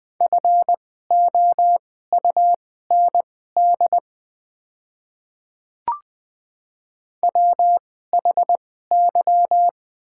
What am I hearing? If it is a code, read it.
Morse code, heard as a keyed tone, about 20 wpm, spaced as for 14 wpm, 700 Hz. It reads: FOUND WHY
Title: 2x Character Spacing - Top 500 Words - 20wpm